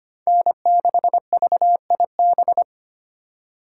Morse code, heard as a keyed tone, about 25 wpm, 700 Hz. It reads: N6VIB
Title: USA Call Signs - 5 Characters - 4x Character Spacing - 25wpm